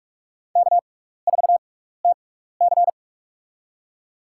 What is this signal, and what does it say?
Morse code, heard as a keyed tone, about 41 wpm, spaced as for 15 wpm, 700 Hz. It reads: K4TC
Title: USA Call Signs - 6x Character Spacing - 45wpm